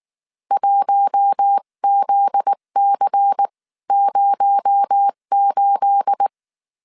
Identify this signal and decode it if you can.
Morse code, heard as a keyed tone, about 19 wpm, spaced as for 16 wpm, 800 Hz. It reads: 1ZC08
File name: CW / DET / 1 9 1500C S B 18wpm 2551